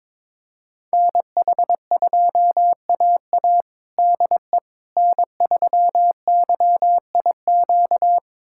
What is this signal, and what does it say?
Morse code, heard as a keyed tone, about 22 wpm, 700 Hz. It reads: NH2AA DE N3YIQ
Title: USA Callsign DE Callsign - 5 Characters - 22wpm